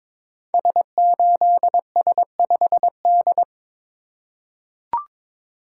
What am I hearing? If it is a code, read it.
Morse code, heard as a keyed tone, about 22 wpm, 700 Hz. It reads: S8S5D
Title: Code Groups - 5 Letter:Numbers - 22wpm